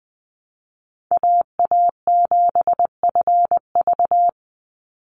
Morse code, heard as a keyed tone, about 20 wpm, 700 Hz. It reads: AA7FV
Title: Call Signs - 20wpm